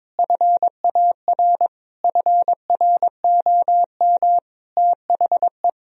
Morse code, heard as a keyed tone, about 22 wpm, 700 Hz. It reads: FAR FROM THE
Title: Sentences from Top 1000 Words - Review Part 1 - 22wpm